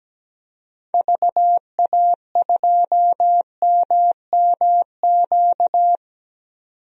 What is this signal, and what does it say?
Morse code, heard as a keyed tone, about 17 wpm, 700 Hz. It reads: VA2MMQ
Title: DXCC-100 Call Signs - 6 Characters - 17wpm